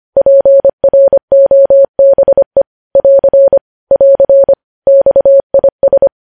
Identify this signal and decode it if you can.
Morse code, heard as a keyed tone, about 25 wpm, 550 Hz. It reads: PROBE <AR> <AR> XIS
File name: podcast / News headlines-1x-25x25